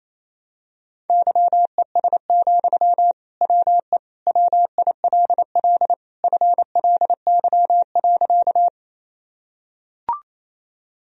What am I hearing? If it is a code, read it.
Morse code, heard as a keyed tone, about 28 wpm, 700 Hz. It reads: YES, WE WILL FLY.
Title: Sentences from Top 1000 Words - Review Part 2 - 28wpm